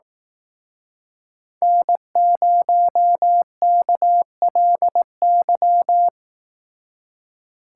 Morse code, heard as a keyed tone, about 18 wpm, 700 Hz. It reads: N0KLY
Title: Speed Racing - USA 1x3 Call Signs 15wpm